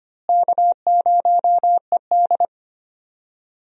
Morse code, heard as a keyed tone, about 25 wpm, 700 Hz. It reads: K0ED
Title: USA Call Signs - 4 Characters - Repeated 3x - 25wpm